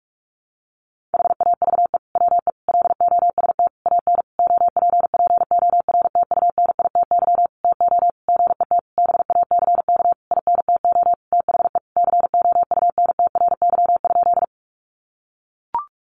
Morse code, heard as a keyed tone, about 44 wpm, 700 Hz. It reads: HAVE WE LOST AN OPPORTUNITY TO GET BACK INTO THE COUNTRY?
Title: Sentences from Top 1400 Words 45wpm